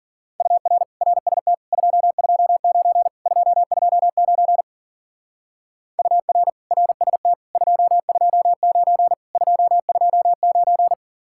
Morse code, heard as a keyed tone, about 52 wpm, 700 Hz. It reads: UR RST 229 229 UR RST 229 229